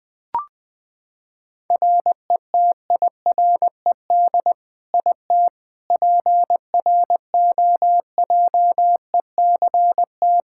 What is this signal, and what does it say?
Morse code, heard as a keyed tone, about 20 wpm, 700 Hz. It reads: RETIRED IT PROJECT
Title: Practice QSO - Exchange 3 - More Info - 20wpm